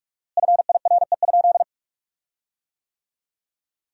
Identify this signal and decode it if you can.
Morse code, heard as a keyed tone, about 41 wpm, 700 Hz. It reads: FIRE?
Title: 4x Word-Spacing - Sentences from Top 400 Words - 45wpm